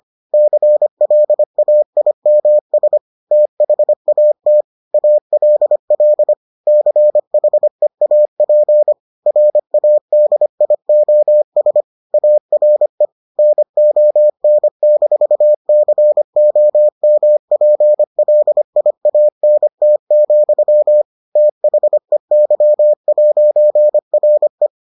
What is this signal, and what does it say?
Morse code, heard as a keyed tone, about 25 wpm, 600 Hz. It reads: CLAIMS THAT ALL CHEAP RADIOS ARE NON-COMPLIANT, THEY'RE